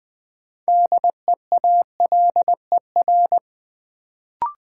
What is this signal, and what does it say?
Morse code, heard as a keyed tone, about 21 wpm, 700 Hz. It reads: DEALER
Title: Sentences from Top 1100 Words - New Word Drill Alt - 20wpm